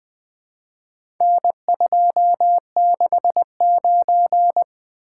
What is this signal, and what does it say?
Morse code, heard as a keyed tone, about 20 wpm, 700 Hz. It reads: N269
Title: Four Letter-Number Combinations - 20wpm